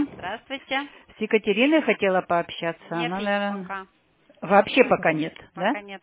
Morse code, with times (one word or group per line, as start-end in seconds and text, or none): none